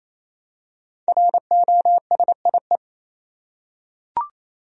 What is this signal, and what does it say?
Morse code, heard as a keyed tone, about 27 wpm, 700 Hz. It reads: ROSIE
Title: USA Common Names - Max 5 Letters - 28wpm